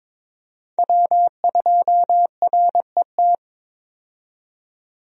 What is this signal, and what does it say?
Morse code, heard as a keyed tone, about 22 wpm, 700 Hz. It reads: W2RET
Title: USA Call Signs - 5 Characters - 22wpm